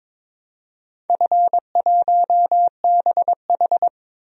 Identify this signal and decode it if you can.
Morse code, heard as a keyed tone, about 23 wpm, 700 Hz. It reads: F1BH
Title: DXCC-100 Call Signs - 4 Characters - 22wpm